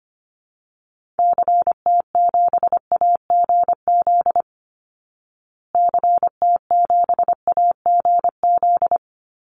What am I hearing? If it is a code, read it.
Morse code, heard as a keyed tone, about 25 wpm, 700 Hz. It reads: CT7AGZ CT7AGZ
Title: Call Signs Repeated 2x - 25wpm